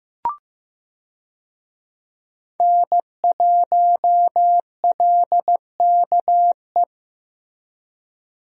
Morse code, heard as a keyed tone, about 16 wpm, 700 Hz. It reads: N1LKE
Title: USA Call Signs - 5 Characters - Repeated 2x - 15wpm